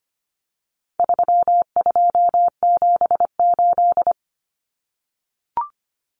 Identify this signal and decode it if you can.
Morse code, heard as a keyed tone, about 25 wpm, 700 Hz. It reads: 3278
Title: Common Cut Numbers - 4-Digit 25wpm